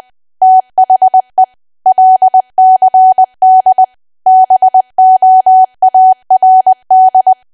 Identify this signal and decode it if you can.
Morse code, heard as a keyed tone, about 20 wpm, 750 Hz. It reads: THE LCD BOARD
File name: cwa / W1AW-Level3 / 20WPM